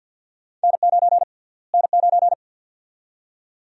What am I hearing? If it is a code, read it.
Morse code, heard as a keyed tone, about 49 wpm, 700 Hz. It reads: N9 N9
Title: USA Call Sign Prefixes - Repeated 2x - 50wpm